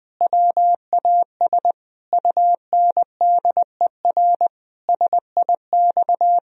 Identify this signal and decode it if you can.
Morse code, heard as a keyed tone, about 20 wpm, 700 Hz. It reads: WAS UNDER SIX
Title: Sentences from Top 1000 Words - Review Part 2 - 20wpm